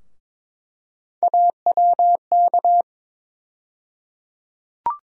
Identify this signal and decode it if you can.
Morse code, heard as a keyed tone, about 22 wpm, 700 Hz. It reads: AWK
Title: USA Call Sign Postfixes - 3 Letters 22wpm